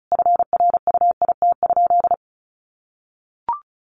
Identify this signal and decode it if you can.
Morse code, heard as a keyed tone, about 34 wpm, 700 Hz. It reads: FRUIT?
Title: Sentences from Top 1500 Words 35wpm